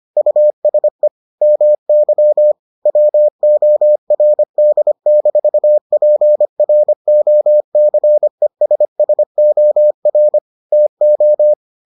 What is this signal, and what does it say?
Morse code, heard as a keyed tone, about 25 wpm, 600 Hz. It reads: USE MY WORD-PROCESSOR TO